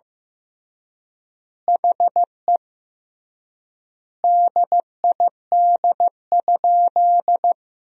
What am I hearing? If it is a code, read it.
Morse code, heard as a keyed tone, about 15 wpm, 700 Hz. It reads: HE DID?
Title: 3x Word-Spacing - Sentences from Top 100 Words - 15wpm